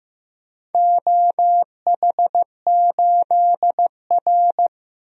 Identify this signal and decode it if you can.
Morse code, heard as a keyed tone, about 15 wpm, 700 Hz. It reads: OH8R
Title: DXCC-100 Call Signs - 4 Characters - 15wpm